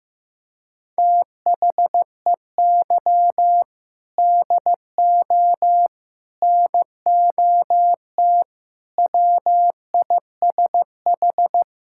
Morse code, heard as a keyed tone, about 15 wpm, 700 Hz. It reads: THEY DO NOT WISH
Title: Sentences from Top 1600 Words 15wpm